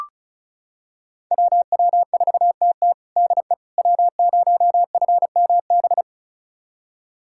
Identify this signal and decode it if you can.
Morse code, heard as a keyed tone, about 36 wpm, 700 Hz. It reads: WW4TT DE W0FMB
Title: USA Callsign DE Callsign - 5 Characters - 35wpm